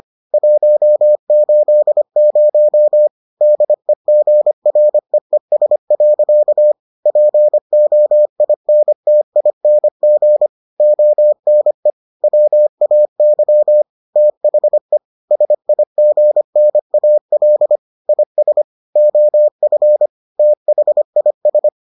Morse code, heard as a keyed tone, about 25 wpm, 600 Hz. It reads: 180 DEGREES. POINTING ONE WAY THE SIGNAL IS OF THIS